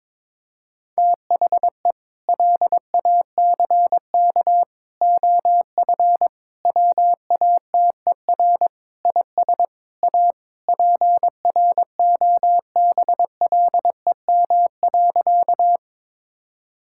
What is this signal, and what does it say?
Morse code, heard as a keyed tone, about 22 wpm, 700 Hz. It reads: THE LACK OF WATER IS A PROBLEM.
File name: practice / 22wpm / Sentences with 3-Word Phrases - 22wpm